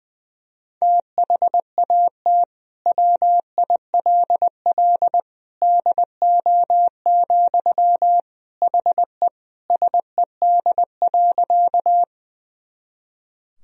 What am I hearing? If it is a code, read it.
Morse code, heard as a keyed tone, about 20 wpm, 700 Hz. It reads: THAT WILL DO, HE SED.
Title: Sentences from Top 100 Words and Abbreviations 20wpm